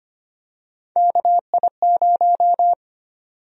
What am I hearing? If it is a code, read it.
Morse code, heard as a keyed tone, about 25 wpm, 700 Hz. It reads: KI0
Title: USA Call Sign Prefixes - 25wpm